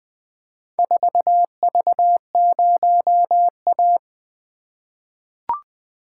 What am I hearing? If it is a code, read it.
Morse code, heard as a keyed tone, about 20 wpm, 700 Hz. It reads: 4V0A
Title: Four Letter-Number Combinations - 20wpm